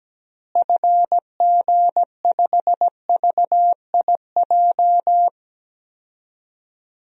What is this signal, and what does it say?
Morse code, heard as a keyed tone, about 17 wpm, 700 Hz. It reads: FG5VIJ
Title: DXCC-100 Call Signs - 6 Characters - Repeated 2x - 17wpm